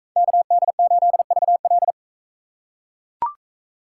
Morse code, heard as a keyed tone, about 39 wpm, 700 Hz. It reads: KD8VL